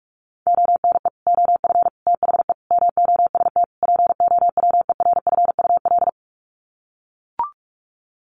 Morse code, heard as a keyed tone, about 44 wpm, 700 Hz. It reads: ONE OF THE MOST POWERFUL
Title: Sets of 5 Words Encore 45wpm